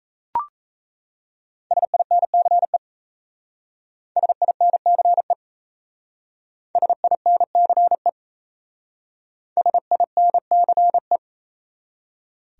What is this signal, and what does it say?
Morse code, heard as a keyed tone, about 34 wpm, 700 Hz. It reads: SINCE SINCE SINCE SINCE
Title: Speed-Racing Top 500 Words 28wpm